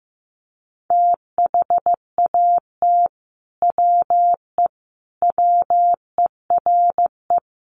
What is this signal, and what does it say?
Morse code, heard as a keyed tone, about 15 wpm, 700 Hz. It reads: THAT WE WERE
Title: Sets of 5 Words Encore2 15wpm